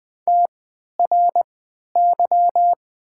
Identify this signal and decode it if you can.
Morse code, heard as a keyed tone, about 20 wpm, spaced as for 11 wpm, 700 Hz. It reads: TRY
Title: 3x Character Spacing - Top 500 Words - 20wpm